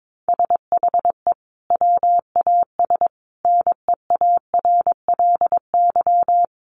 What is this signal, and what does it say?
Morse code, heard as a keyed tone, about 22 wpm, 700 Hz. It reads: SHE WAS NEARLY
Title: Sentences from Top 1300 Words 22wpm